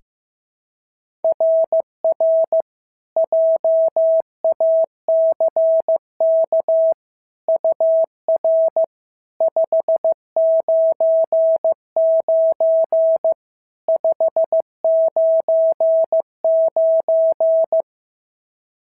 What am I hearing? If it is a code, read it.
Morse code, heard as a keyed tone, about 15 wpm, 650 Hz. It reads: RR JACK UR 599 599
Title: SOTA QSO Practice 15wpm